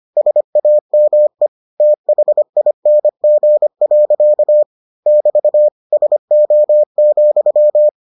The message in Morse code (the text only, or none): SAME THING. <BT> SO,